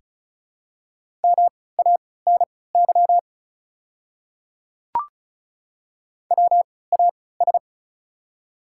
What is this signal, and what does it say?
Morse code, heard as a keyed tone, about 36 wpm, spaced as for 20 wpm, 700 Hz. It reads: MANY WAS
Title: 3x Character Spacing - Top 100 Words - 35wpm